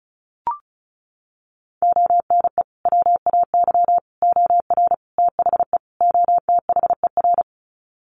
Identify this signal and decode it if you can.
Morse code, heard as a keyed tone, about 34 wpm, 700 Hz. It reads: ONE WAY OR THE OTHER
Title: Sets of 5 Words 35wpm